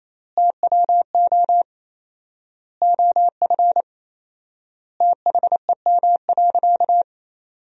Morse code, heard as a keyed tone, about 28 wpm, 700 Hz. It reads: TWO OF THEM.
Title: 4x Word-Spacing - Sentences from Top 100 Words - 28wpm